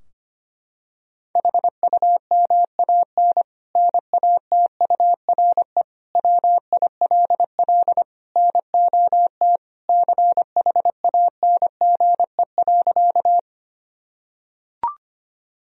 Morse code, heard as a keyed tone, about 25 wpm, 700 Hz. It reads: HUMAN NATURE WILL NOT CHANGE.
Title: Sentences from Top 1000 Words - Review Part 3 - 25wpm